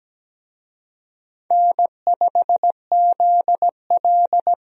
Morse code, heard as a keyed tone, about 17 wpm, 700 Hz. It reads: N5ZL
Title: USA Call Signs - 4 Characters - Repeated 2x - 17wpm